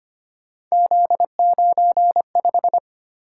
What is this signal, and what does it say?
Morse code, heard as a keyed tone, about 25 wpm, 700 Hz. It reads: Z95
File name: Three Letter-Number Combinations - 25wpm